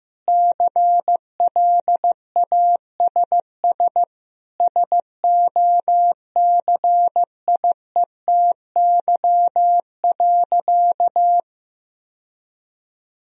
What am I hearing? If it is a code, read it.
Morse code, heard as a keyed tone, about 15 wpm, 700 Hz. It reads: CLASS SOCIETY.
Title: Sentences from Top 1000 Words - Review Part 4 - 15wpm